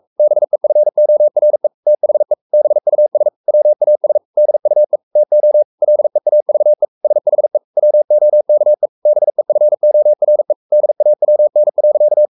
Morse code, heard as a keyed tone, about 44 wpm, 600 Hz. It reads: BEFORE THE BUS WAS DUE TO LEAVE SHE WOKE BEFORE DAWN.
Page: Romanen Never af Ken Follet på engelsk.